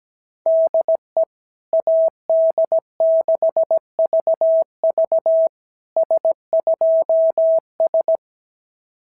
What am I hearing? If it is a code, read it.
Morse code, heard as a keyed tone, about 17 wpm, 690 Hz. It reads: DE AD6VV S2S
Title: SOTA QSO Practice 17wpm